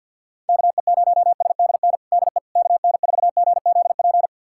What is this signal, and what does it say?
Morse code, heard as a keyed tone, about 52 wpm, 700 Hz. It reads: KE0SDN DE KN4GZP
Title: USA Callsign DE Callsign - 6 Characters - 50wpm